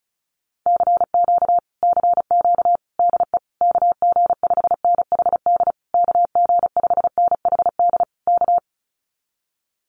Morse code, heard as a keyed tone, about 34 wpm, 700 Hz. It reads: CQ CQ DE KG5NHD KG5NHD K
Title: Common QSO Phrases Encore 35wpm